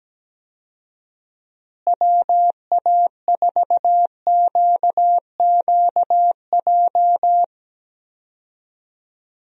Morse code, heard as a keyed tone, about 17 wpm, 700 Hz. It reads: WA4QQJ
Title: USA Call Signs - 6 Characters - Repeated 2x - 17wpm